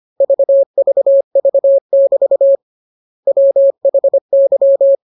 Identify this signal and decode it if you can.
Morse code, heard as a keyed tone, about 25 wpm, 550 Hz. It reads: VVV<BT> WHY
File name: jokes / joke 1 25 0000